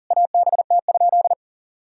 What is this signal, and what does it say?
Morse code, heard as a keyed tone, about 37 wpm, 700 Hz. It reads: ABT?
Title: Sentences from Top 400 Words and Abbreviations 40wpm